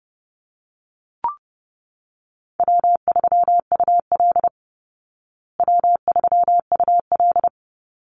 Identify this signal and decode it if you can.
Morse code, heard as a keyed tone, about 30 wpm, 700 Hz. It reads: W3UL W3UL
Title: Easy Call Signs Repeated 2x - 30wpm